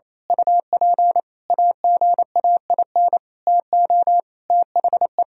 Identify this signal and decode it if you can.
Morse code, heard as a keyed tone, about 28 wpm, 700 Hz. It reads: UP AGAIN TO THE
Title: Sentences from Top 1000 Words - Review Part 3 - 28wpm